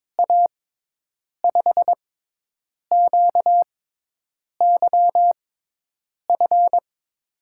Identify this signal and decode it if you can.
Morse code, heard as a keyed tone, about 21 wpm, spaced as for 8 wpm, 700 Hz. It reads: A5QYF
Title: USA Call Signs - 6 Characters - 6x Character Spacing - 22wpm